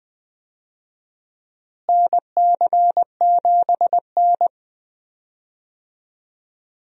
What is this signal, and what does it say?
Morse code, heard as a keyed tone, about 20 wpm, 700 Hz. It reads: NC7N
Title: USA Call Signs - 4 Characters - Repeated 3x - 20wpm